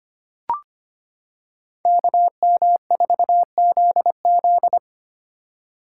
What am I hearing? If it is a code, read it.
Morse code, heard as a keyed tone, about 25 wpm, 700 Hz. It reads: KM4ZZ
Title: K1USN SST QSO Practice - 25wpm